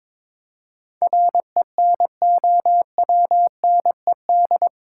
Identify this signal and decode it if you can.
Morse code, heard as a keyed tone, about 22 wpm, 700 Hz. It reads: RENOWNED
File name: practice / 22wpm / Eight Letter Words - 22wpm